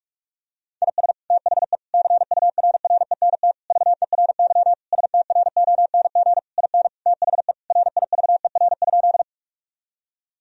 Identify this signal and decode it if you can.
Morse code, heard as a keyed tone, about 46 wpm, 700 Hz. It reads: IS THE CURRENT VERY STRONG IN THE RIVER?
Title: Sentences from Top 1000 Words - Review Part 4 - 45wpm